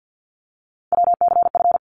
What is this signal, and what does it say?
Morse code, heard as a keyed tone, about 50 wpm, 700 Hz. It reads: WCF
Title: ARRL Sections 50wpm